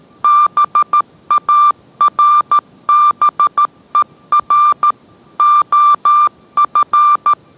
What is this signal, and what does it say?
Morse code, heard as a keyed tone, about 13 wpm, 1180 Hz. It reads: BARBER OF